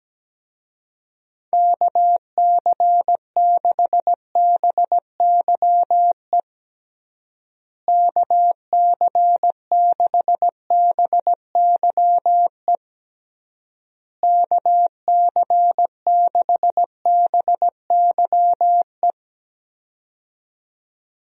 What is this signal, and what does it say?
Morse code, heard as a keyed tone, about 17 wpm, 700 Hz. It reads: KC6BYE KC6BYE KC6BYE
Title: USA Call Signs - 6 Characters - Repeated 3x - 17wpm